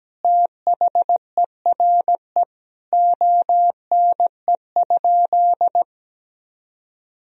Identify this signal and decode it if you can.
Morse code, heard as a keyed tone, about 17 wpm, 700 Hz. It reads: THERE ONE?
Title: Sentences from Top 100 Words and Abbreviations 17wpm